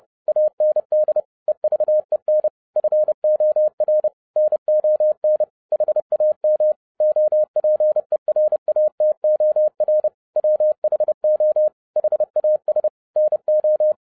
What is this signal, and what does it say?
Morse code, heard as a keyed tone, about 30 wpm, 610 Hz. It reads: AND EVEN FOR NON HAM OPERATOR WHO HAS NO